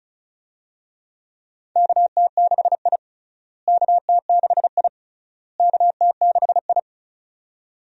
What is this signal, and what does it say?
Morse code, heard as a keyed tone, about 36 wpm, 700 Hz. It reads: KT6I KT6I KT6I
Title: USA Call Signs - 4 Characters - Repeated 3x - 35wpm